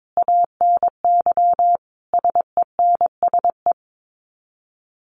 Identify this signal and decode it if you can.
Morse code, heard as a keyed tone, about 22 wpm, 700 Hz. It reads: ANY SENSE
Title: Sets of 5 Words Encore2 22wpm